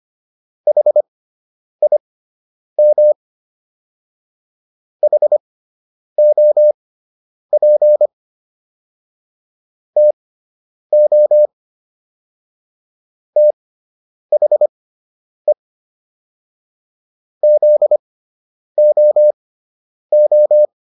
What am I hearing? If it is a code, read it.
Morse code, heard as a keyed tone, about 25 wpm, spaced as for 9 wpm, 600 Hz. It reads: HIM HOP TO THE ZOO